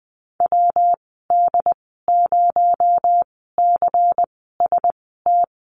Morse code, heard as a keyed tone, about 20 wpm, spaced as for 14 wpm, 700 Hz. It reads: WD0CST
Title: 245 Lesson 39g - Letter z - Call Signs 20wpm